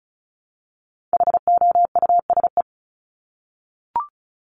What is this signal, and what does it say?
Morse code, heard as a keyed tone, about 34 wpm, 700 Hz. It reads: HOUSE
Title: ING Suffix - Top 100 Words with Base Words 35wpm